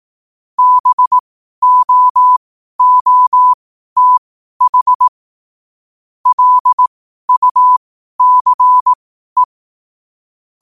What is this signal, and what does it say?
Morse code, heard as a keyed tone, about 18 wpm, spaced as for 12 wpm, 1000 Hz. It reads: BOOTH LUCE
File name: qotdfiles / 12wpm